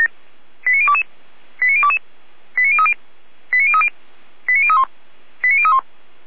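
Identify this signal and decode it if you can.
Five-tone selective call (ZVEI), heard as quick runs of five tones, tones of about 70 ms. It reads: 89022 89022 89030 89030 89031 89031
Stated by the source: ZVEI five-tone call 89014 received by SDR.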